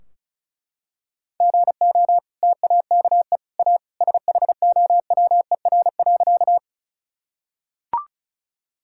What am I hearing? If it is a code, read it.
Morse code, heard as a keyed tone, about 34 wpm, 700 Hz. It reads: GO TAKE A SHOWER.
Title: Sentences from Top 1600 Words 35wpm